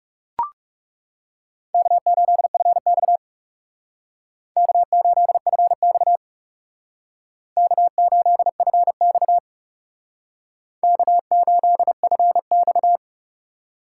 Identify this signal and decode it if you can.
Morse code, heard as a keyed tone, about 36 wpm, 700 Hz. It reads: K8FX K8FX K8FX K8FX
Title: Speed Racing - USA 1x2 Call Signs 30wpm